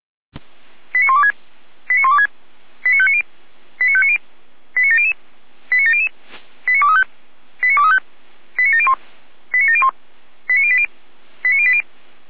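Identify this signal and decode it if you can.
Five-tone selective call (ZVEI), heard as quick runs of five tones, tones of about 70 ms. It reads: 89126 89126 89590 89590 89700 89700 89235 89235 89701 89701 89080 89080
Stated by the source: ZVEI five-tone selective call received via SDR.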